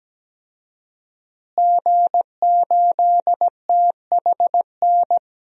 Morse code, heard as a keyed tone, about 17 wpm, 700 Hz. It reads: G8THN
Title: DXCC-100 Call Signs - 5 Characters - Repeated 2x - 17wpm